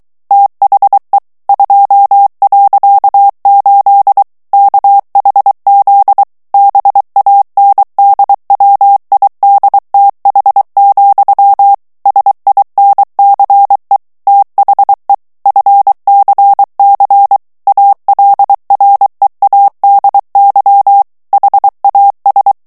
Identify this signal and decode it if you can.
Morse code, heard as a keyed tone, about 23 wpm, 800 Hz. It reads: THE 2.8 KHZ BANDWIDTH, SINCE THE FCC ALREADY HAS
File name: ANGLAIS texte clair 3 vitesse 22 mots minute